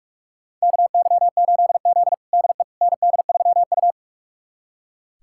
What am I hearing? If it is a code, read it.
Morse code, heard as a keyed tone, about 42 wpm, 700 Hz. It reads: KY8Z DE ND3U
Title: USA Callsign DE Callsign - 4 Characters - 45wpm